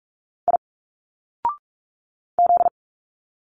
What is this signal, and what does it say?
Morse code, heard as a keyed tone, about 46 wpm, 700 Hz. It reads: I Z
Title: Single Letters 45wpm